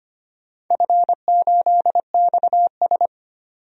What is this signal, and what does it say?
Morse code, heard as a keyed tone, about 25 wpm, 700 Hz. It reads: F8XS
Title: DXCC-100 Call Signs - 4 Characters - 25wpm